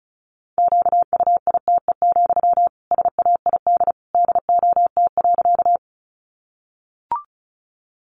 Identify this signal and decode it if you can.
Morse code, heard as a keyed tone, about 34 wpm, 700 Hz. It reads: QUITE, SAID DOT.
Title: Sentences from Top 1500 Words 35wpm